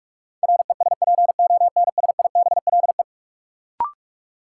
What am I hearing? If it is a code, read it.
Morse code, heard as a keyed tone, about 46 wpm, 700 Hz. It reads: RESPONSIBLE
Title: Sentences from Top 1000 Words - Review Part 7 - New Word Drill - 45wpm